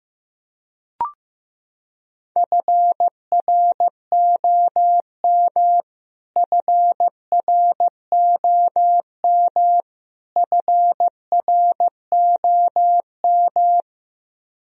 Morse code, heard as a keyed tone, about 16 wpm, 700 Hz. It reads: FROM FROM FROM
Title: Top 100 Words Repeated 15wpm